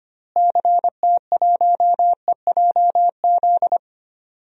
Morse code, heard as a keyed tone, about 25 wpm, 700 Hz. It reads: CT1EJZ
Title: DXCC-100 Call Signs - 6 Characters - Repeated 2x - 25wpm